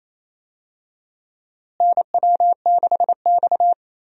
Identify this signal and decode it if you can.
Morse code, heard as a keyed tone, about 28 wpm, 700 Hz. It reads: NW6X
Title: USA Call Signs - 4 Characters - Repeated 2x - 28wpm